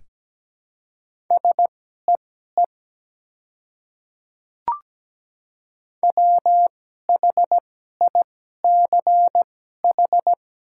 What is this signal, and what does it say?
Morse code, heard as a keyed tone, about 17 wpm, spaced as for 12 wpm, 700 Hz. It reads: SEE WHICH
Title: 2x Character Spacing - Top 100 Words - 17wpm